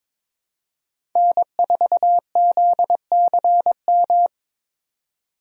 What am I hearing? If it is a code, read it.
Morse code, heard as a keyed tone, about 21 wpm, 700 Hz. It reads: N4ZCM
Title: USA Call Signs - 5 Characters - 22wpm